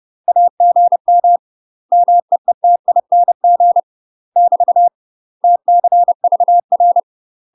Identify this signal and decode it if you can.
Morse code, heard as a keyed tone, about 30 wpm, 700 Hz. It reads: AGM MEETING <BT> TCVR
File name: UK MASTER QSOs TRAINING / 30WPM / UK MASTER QSOs 0019